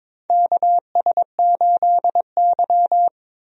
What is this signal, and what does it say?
Morse code, heard as a keyed tone, about 22 wpm, 700 Hz. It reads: KS8Y
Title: USA Call Signs - 4 Characters - 22wpm